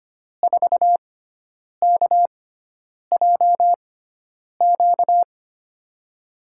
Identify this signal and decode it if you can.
Morse code, heard as a keyed tone, about 25 wpm, spaced as for 9 wpm, 700 Hz. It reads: 4KJQ
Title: USA Call Signs - 6x Character Spacing - 25wpm